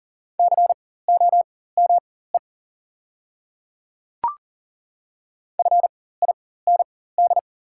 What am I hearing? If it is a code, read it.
Morse code, heard as a keyed tone, about 40 wpm, spaced as for 19 wpm, 690 Hz. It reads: COME FIND
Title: 4x Character Spacing - Top 200 Words - 40wpm